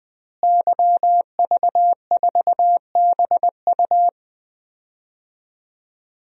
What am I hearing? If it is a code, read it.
Morse code, heard as a keyed tone, about 21 wpm, 700 Hz. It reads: YV4BU
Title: DXCC-100 Call Signs - 5 Characters - Repeated 3x - 20wpm